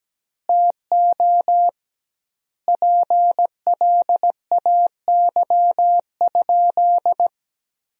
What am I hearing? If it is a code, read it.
Morse code, heard as a keyed tone, about 17 wpm, 700 Hz. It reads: TO PLAY?
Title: 2x Word-Spacing - Sentences from Top 400 Words - 17wpm